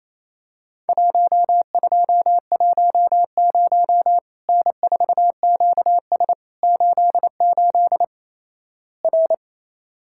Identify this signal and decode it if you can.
Morse code, heard as a keyed tone, about 28 wpm, 700 Hz. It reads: 1210 N4QS 88 R
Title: Worked All Europe DX Contest - QTC Exchange - 28wpm